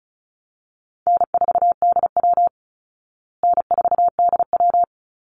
Morse code, heard as a keyed tone, about 34 wpm, 700 Hz. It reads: N4DW N4DW
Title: Easy Call Signs Repeated 2x - 35wpm